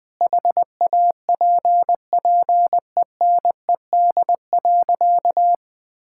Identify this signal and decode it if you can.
Morse code, heard as a keyed tone, about 21 wpm, 700 Hz. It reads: HAPPENED.
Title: Sentences with 5-Word Phrases - Spoken Beginning - 20wpm